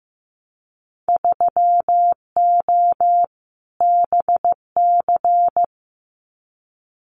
Morse code, heard as a keyed tone, about 15 wpm, 700 Hz. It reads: 3O BC
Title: Winter Field Day Exchanges Encore 15wpm